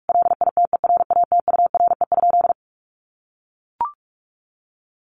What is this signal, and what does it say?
Morse code, heard as a keyed tone, about 44 wpm, 700 Hz. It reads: LITERATURE?
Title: Sentences from Top 600 Words 45wpm